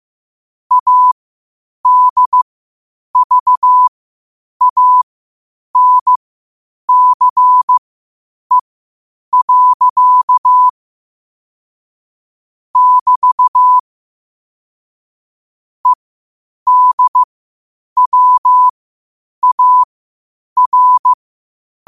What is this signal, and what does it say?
Morse code, heard as a keyed tone, about 15 wpm, spaced as for 8 wpm, 1000 Hz. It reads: ADVANCE. <BT> EDWAR
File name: qotdfiles / 8wpm